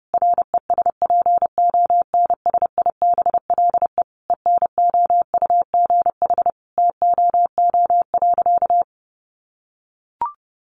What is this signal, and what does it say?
Morse code, heard as a keyed tone, about 30 wpm, 700 Hz. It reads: RESPONSIBLE ENOUGH TOO.
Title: Easy Phrases 30wpm